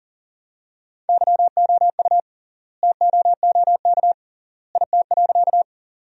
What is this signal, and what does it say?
Morse code, heard as a keyed tone, about 40 wpm, 700 Hz. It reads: YOU TOOK IT.
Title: 3x Word-Spacing - Sentences from Top 300 Words - 40wpm